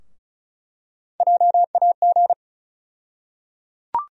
Morse code, heard as a keyed tone, about 34 wpm, 700 Hz. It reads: JAG
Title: USA Call Sign Postfixes - 3 Letters 35wpm